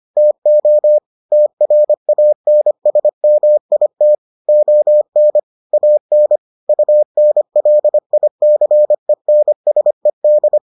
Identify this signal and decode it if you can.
Morse code, heard as a keyed tone, about 25 wpm, 600 Hz. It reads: TO TRANSMIT ON AN UNLICENSED